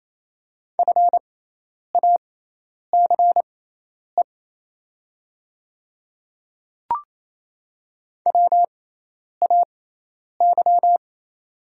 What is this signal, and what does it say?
Morse code, heard as a keyed tone, about 28 wpm, spaced as for 10 wpm, 700 Hz. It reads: FACE WAY
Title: 6x Character Spacing - Top 500 Words - 28wpm